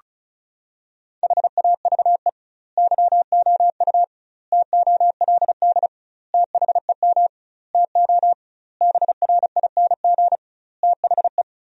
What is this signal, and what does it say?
Morse code, heard as a keyed tone, about 34 wpm, 700 Hz. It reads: HAVE YOU TOLD THEM TO BRING THE
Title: 2x Word-Spacing - Sentences from Top 500 Words - 35wpm